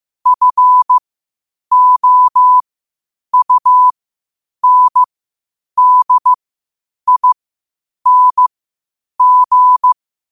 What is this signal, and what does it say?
Morse code, heard as a keyed tone, about 15 wpm, spaced as for 8 wpm, 1000 Hz. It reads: FOUNDING
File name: qotdfiles / 8wpm